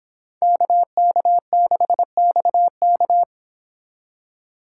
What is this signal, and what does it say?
Morse code, heard as a keyed tone, about 31 wpm, 700 Hz. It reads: KK6XK